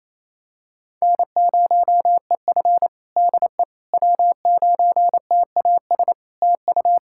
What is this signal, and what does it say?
Morse code, heard as a keyed tone, about 28 wpm, 700 Hz. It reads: N0EF DE W9TAS TU
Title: Practice QSO - Exchange 4 - Ending - 28wpm